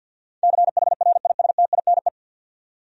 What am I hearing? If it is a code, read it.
Morse code, heard as a keyed tone, about 49 wpm, 700 Hz. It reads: CHRISTINE
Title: USA Common Names - 50wpm